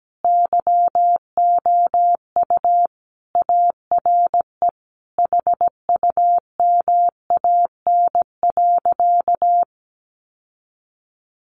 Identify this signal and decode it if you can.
Morse code, heard as a keyed tone, about 17 wpm, 700 Hz. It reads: YOU ARE HUMAN.
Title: Sentences from Top 600 Words 17wpm